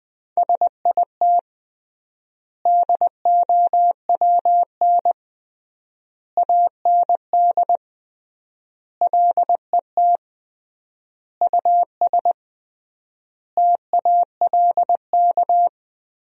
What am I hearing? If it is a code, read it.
Morse code, heard as a keyed tone, about 21 wpm, 700 Hz. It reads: SIT DOWN AND LET US TALK
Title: 3x Word-Spacing - Sentences from Top 500 Words - 20wpm